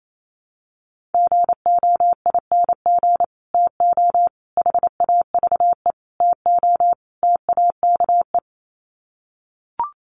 Morse code, heard as a keyed tone, about 28 wpm, 700 Hz. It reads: GOING TO HAVE TO TAKE
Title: Sets of 5 Words Encore 28wpm